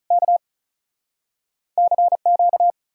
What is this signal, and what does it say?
Morse code, heard as a keyed tone, about 46 wpm, 700 Hz. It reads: K CQ